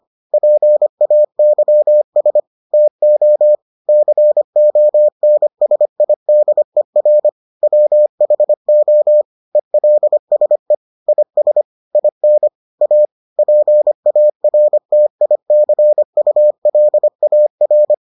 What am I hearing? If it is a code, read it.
Morse code, heard as a keyed tone, about 25 wpm, 600 Hz. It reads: PAYS TO CONSIDER WHO ELSE IS IN A PARTICULAR